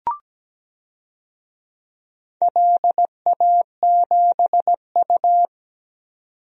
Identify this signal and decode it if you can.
Morse code, heard as a keyed tone, about 17 wpm, 700 Hz. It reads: LA7U
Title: DXCC-100 Call Signs - 4 Characters - 17wpm